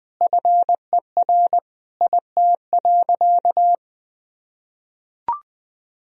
Morse code, heard as a keyed tone, about 20 wpm, 700 Hz. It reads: FER IT.
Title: Sentences from Top 300 Words and Abbreviations 20wpm